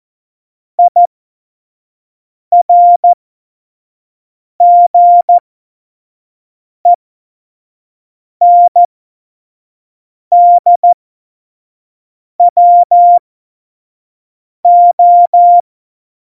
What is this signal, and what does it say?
Morse code, heard as a keyed tone, about 14 wpm, spaced as for 5 wpm, 700 Hz. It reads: IRGENDWO
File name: Morsekurs / Witze / Witz 04. Juni 2019, 25 F70